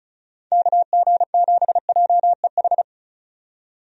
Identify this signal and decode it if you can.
Morse code, heard as a keyed tone, about 34 wpm, 700 Hz. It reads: KG7JEH
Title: USA Call Signs - 6 Characters - 35wpm